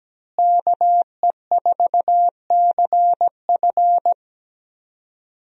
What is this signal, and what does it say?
Morse code, heard as a keyed tone, about 17 wpm, 700 Hz. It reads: KE4CF
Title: USA Call Signs - 5 Characters - Repeated 2x - 17wpm